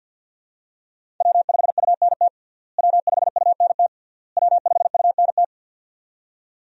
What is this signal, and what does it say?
Morse code, heard as a keyed tone, about 52 wpm, 700 Hz. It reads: W5UNT W5UNT W5UNT
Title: USA Call Signs - 5 Characters - Repeated 3x - 50wpm